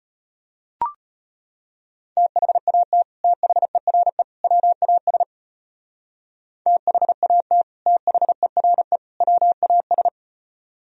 Morse code, heard as a keyed tone, about 33 wpm, 700 Hz. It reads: THAT THERE WAS THAT THERE WAS